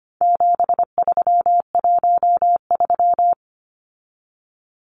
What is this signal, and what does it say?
Morse code, heard as a keyed tone, about 25 wpm, 700 Hz. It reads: 7313
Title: Numbers - 4-Digit 25wpm